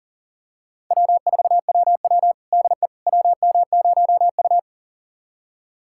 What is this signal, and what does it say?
Morse code, heard as a keyed tone, about 40 wpm, 700 Hz. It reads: W4WW DE WM0U
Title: USA Callsign DE Callsign - 4 Characters - 40wpm